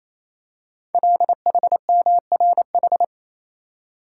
Morse code, heard as a keyed tone, about 27 wpm, 700 Hz. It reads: LHMRH
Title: Code Groups - 5 Letters - 28wpm